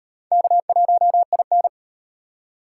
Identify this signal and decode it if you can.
Morse code, heard as a keyed tone, about 38 wpm, 700 Hz. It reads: K1IN